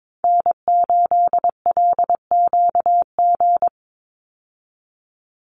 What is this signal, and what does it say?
Morse code, heard as a keyed tone, about 22 wpm, 700 Hz. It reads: N8LQG
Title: USA Call Signs 22wpm